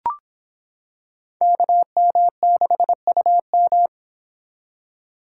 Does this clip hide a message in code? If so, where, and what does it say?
Yes: Morse code from 1.4 to 3.9 s, heard as a keyed tone, about 26 wpm, 700 Hz. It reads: KM6UM